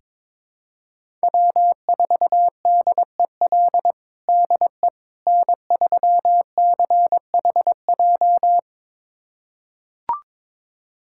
Morse code, heard as a keyed tone, about 22 wpm, 700 Hz. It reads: W4DEL DE N3CHJ
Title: USA Callsign DE Callsign - 5 Characters - 22wpm